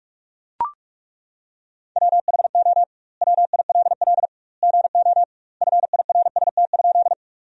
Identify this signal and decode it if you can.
Morse code, heard as a keyed tone, about 42 wpm, 700 Hz. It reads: WHO WILL GO FIRST?
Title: 2x Word-Spacing - Sentences from Top 100 Words - 45wpm